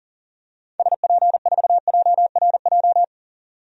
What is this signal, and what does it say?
Morse code, heard as a keyed tone, about 40 wpm, 700 Hz. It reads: SP4JRJ
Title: DXCC-100 Call Signs - 6 Characters - 40wpm